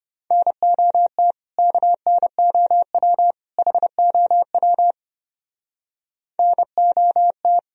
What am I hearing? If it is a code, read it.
Morse code, heard as a keyed tone, about 32 wpm, 700 Hz. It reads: NOT KNOW HOW NOT